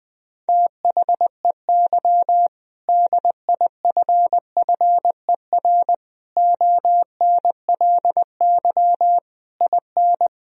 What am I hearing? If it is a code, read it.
Morse code, heard as a keyed tone, about 20 wpm, 700 Hz. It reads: THEY DIFFER ONLY IN
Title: Sentences from Top 1000 Words - Review Part 2 - 20wpm